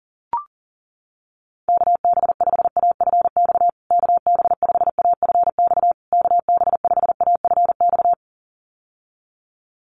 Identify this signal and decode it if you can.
Morse code, heard as a keyed tone, about 40 wpm, 700 Hz. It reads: KB5AFX KB5AFX KB5AFX
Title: USA Call Signs Repeated 3x 40wpm